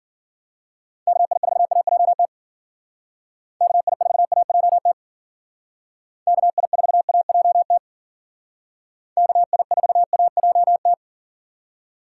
Morse code, heard as a keyed tone, about 48 wpm, 700 Hz. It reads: KI4AJT KI4AJT KI4AJT KI4AJT
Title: Speed Racing - USA Call Signs 40wpm